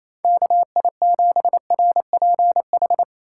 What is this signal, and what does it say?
Morse code, heard as a keyed tone, about 28 wpm, 700 Hz. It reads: KI7RPH
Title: USA Call Signs - 6 Characters - 2x Character Spacing - 28wpm